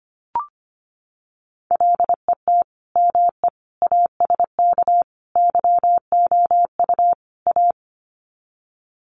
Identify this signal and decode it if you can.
Morse code, heard as a keyed tone, about 25 wpm, 700 Hz. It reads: LET ME ASK YOU A
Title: Sets of 5 Words Encore 25wpm